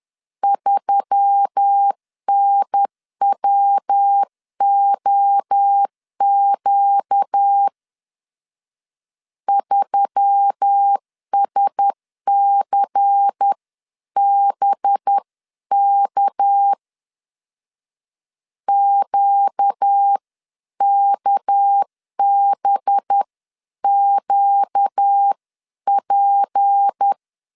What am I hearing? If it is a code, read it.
Morse code, heard as a keyed tone, about 10 wpm, spaced as for 9 wpm, 800 Hz. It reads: 3NWOQ 3SCBK QKBQP
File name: CW / DET / 1 6 1500C S C 10wpm 4252